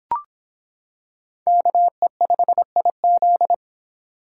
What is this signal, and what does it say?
Morse code, heard as a keyed tone, about 25 wpm, 700 Hz. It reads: KE5IZ